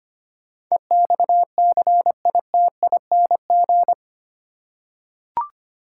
Morse code, heard as a keyed tone, about 25 wpm, 700 Hz. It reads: EXCITING
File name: Eight Letter Words - 25wpm